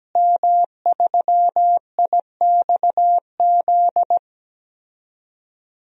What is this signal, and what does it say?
Morse code, heard as a keyed tone, about 17 wpm, 700 Hz. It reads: M3IXZ
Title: DXCC-100 Call Signs - 5 Characters - Repeated 3x - 17wpm